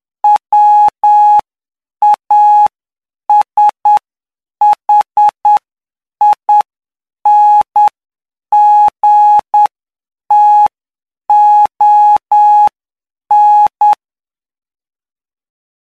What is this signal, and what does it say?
Morse code, heard as a keyed tone, about 9 wpm, spaced as for 7 wpm, 840 Hz. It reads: WASHINGTON
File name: cw / py2lcd / 09-PY2LCD-Li��o 18